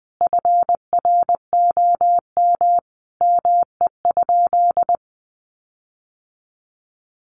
Morse code, heard as a keyed tone, about 20 wpm, 700 Hz. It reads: FROM ME?
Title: Sentences from Top 1500 Words 20wpm